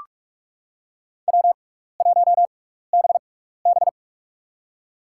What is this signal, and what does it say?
Morse code, heard as a keyed tone, about 43 wpm, spaced as for 15 wpm, 700 Hz. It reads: W1BB
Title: USA Call Signs - 4 Characters - 6x Character Spacing - 45wpm